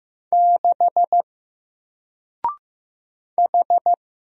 Morse code, heard as a keyed tone, about 15 wpm, 700 Hz. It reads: 6 H
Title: Single Letter-Number Alt 15wpm